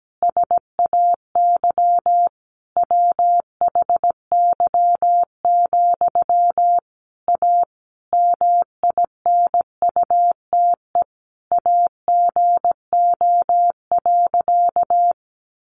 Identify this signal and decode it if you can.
Morse code, heard as a keyed tone, about 17 wpm, 700 Hz. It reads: SAY WHY, A MINUTE AGO.
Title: Sentences from Top 500 Words 17wpm